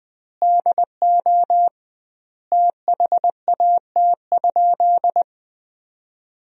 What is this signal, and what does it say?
Morse code, heard as a keyed tone, about 21 wpm, 700 Hz. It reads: DO THAT?
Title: 2x Word-Spacing - Sentences from Top 100 Words - 20wpm